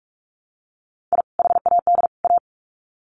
Morse code, heard as a keyed tone, about 44 wpm, 700 Hz. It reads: I HAD A
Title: Sets of 3 Words 45wpm